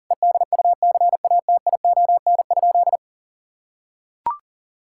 Morse code, heard as a keyed tone, about 40 wpm, 700 Hz. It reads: EDUCATION?
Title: Sentences from Top 1000 Words - Review Part 6 - 40wpm